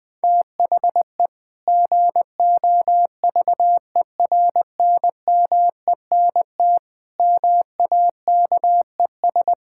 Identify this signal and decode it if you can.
Morse code, heard as a keyed tone, about 20 wpm, 700 Hz. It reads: THE GOVERNMENT MAKES
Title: Sentences with 2-Word Phrases - 20wpm